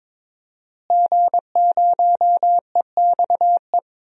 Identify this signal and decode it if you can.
Morse code, heard as a keyed tone, about 22 wpm, 700 Hz. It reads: G0EXE
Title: DXCC-100 Call Signs - 5 Characters - Repeated 2x - 22wpm